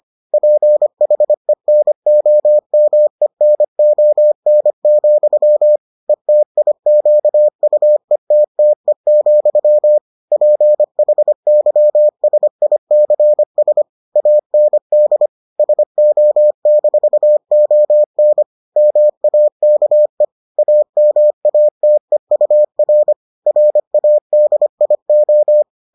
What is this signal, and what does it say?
Morse code, heard as a keyed tone, about 25 wpm, 600 Hz. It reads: PHENOMENON, ETIQUETTE, PHYSICS AND SO-ON MAKE AMATEUR RADIO